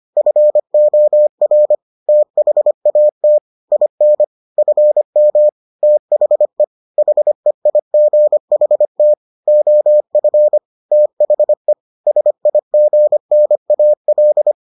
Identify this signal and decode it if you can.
Morse code, heard as a keyed tone, about 25 wpm, 600 Hz. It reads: FOR THAT IN FM THE HEIGHT OF THE SIGNAL